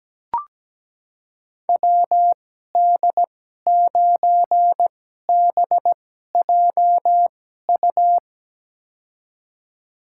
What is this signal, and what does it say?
Morse code, heard as a keyed tone, about 17 wpm, spaced as for 12 wpm, 700 Hz. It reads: WD9BJU
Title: USA Call Signs - 6 Characters - 2x Character Spacing - 17wpm